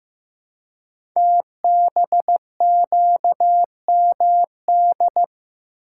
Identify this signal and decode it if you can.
Morse code, heard as a keyed tone, about 15 wpm, 700 Hz. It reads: TBQMD
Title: Code Groups - 5 Letters - 15wpm